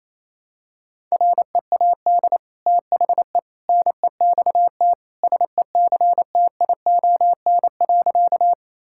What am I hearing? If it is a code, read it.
Morse code, heard as a keyed tone, about 28 wpm, 700 Hz. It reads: READ THE NEXT SECTION.
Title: Sentences from Top 1000 Words - Review Part 5 - 28wpm